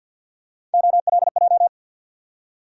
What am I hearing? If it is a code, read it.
Morse code, heard as a keyed tone, about 53 wpm, 700 Hz. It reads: OLJ
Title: USA Call Sign Postfixes - 3 Letters 50wpm